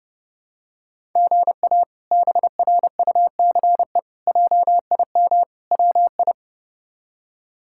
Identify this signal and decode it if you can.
Morse code, heard as a keyed tone, about 30 wpm, 700 Hz. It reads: GA BRUCE JIM WI
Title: K1USN SST QSO Practice - 30wpm